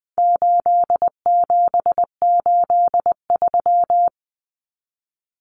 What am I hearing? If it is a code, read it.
Morse code, heard as a keyed tone, about 20 wpm, 700 Hz. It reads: 8783
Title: Numbers - 4-Digit 20wpm